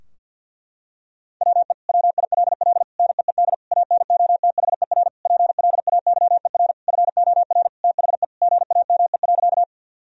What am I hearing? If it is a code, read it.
Morse code, heard as a keyed tone, about 50 wpm, 700 Hz. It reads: WE WILL NEED ANOTHER PLAYER FOR THE GAME.
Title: Sentences from Top 1000 Words - Review Part 4 - 50wpm